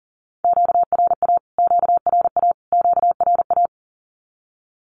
Morse code, heard as a keyed tone, about 40 wpm, 700 Hz. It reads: QRA QRA QRA
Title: Q-Codes Repeated 3x 40wpm